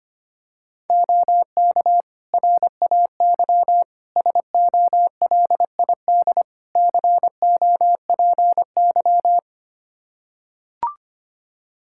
Morse code, heard as a keyed tone, about 25 wpm, 700 Hz. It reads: OK RAY SOLID COPY
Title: Practice QSO - Exchange 3 - More Info - 25wpm